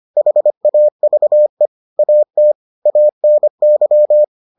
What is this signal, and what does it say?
Morse code, heard as a keyed tone, about 25 wpm, 600 Hz. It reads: HAVE AT ANY